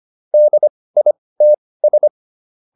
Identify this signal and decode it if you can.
Morse code, heard as a keyed tone, about 25 wpm, spaced as for 18 wpm, 600 Hz. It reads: DITS